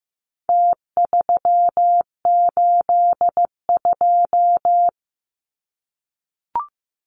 Morse code, heard as a keyed tone, about 15 wpm, 700 Hz. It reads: T382
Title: Numbers - Contest Style - 4-Digit 15wpm